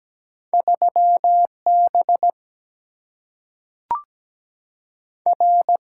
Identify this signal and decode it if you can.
Morse code, heard as a keyed tone, about 17 wpm, 700 Hz. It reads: 3 B R
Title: Single-Letter Single-Number 17wpm